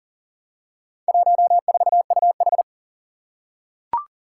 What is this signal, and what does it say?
Morse code, heard as a keyed tone, about 40 wpm, 700 Hz. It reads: 14UH
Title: Four Challenging Letter-Number Combinations 40wpm